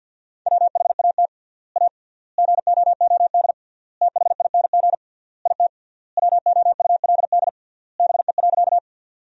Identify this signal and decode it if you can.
Morse code, heard as a keyed tone, about 50 wpm, 700 Hz. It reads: WHAT A GOOD THING IT WOULD BE.
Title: 3x Word-Spacing - Sentences from Top 200 Words - 50wpm